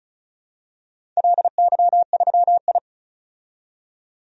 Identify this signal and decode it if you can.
Morse code, heard as a keyed tone, about 36 wpm, 700 Hz. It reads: LY3I
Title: DXCC-100 Call Signs - 4 Characters - 35wpm